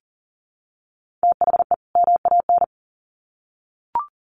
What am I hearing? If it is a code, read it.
Morse code, heard as a keyed tone, about 40 wpm, 700 Hz. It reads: THE MAN
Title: Sets of 2 Words 40wpm